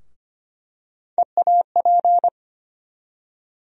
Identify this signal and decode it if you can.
Morse code, heard as a keyed tone, about 25 wpm, 700 Hz. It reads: EAP
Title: USA Call Sign Postfixes - 3 Letters 25wpm